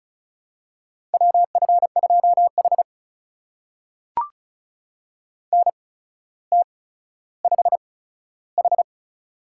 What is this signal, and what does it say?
Morse code, heard as a keyed tone, about 36 wpm, spaced as for 15 wpm, 700 Hz. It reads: WF2H NT5H
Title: USA Call Signs - 4 Characters - 8x Character Spacing - 35wpm